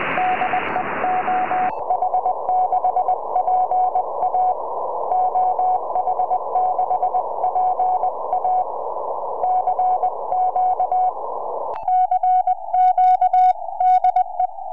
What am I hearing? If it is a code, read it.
Morse code, heard as a keyed tone, about 20 wpm, 740 Hz, with noise. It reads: DE OH6PA OH6PA CQ CQ DE
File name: oh6pa 4m